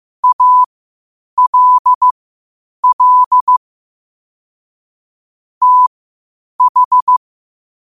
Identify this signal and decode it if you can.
Morse code, heard as a keyed tone, about 15 wpm, spaced as for 8 wpm, 1000 Hz. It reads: ALL TH
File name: qotdfiles / 8wpm